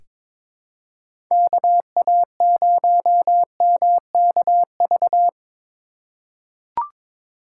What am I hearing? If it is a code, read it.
Morse code, heard as a keyed tone, about 22 wpm, 700 Hz. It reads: KA0MKV
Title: USA Call Signs - 6 Characters - 3x Character Spacing - 22wpm